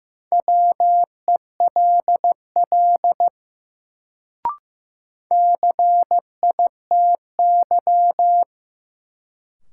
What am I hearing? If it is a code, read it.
Morse code, heard as a keyed tone, about 15 wpm, 700 Hz. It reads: WELL CITY
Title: Sets of 2 Words - New Word Drill - 15wpm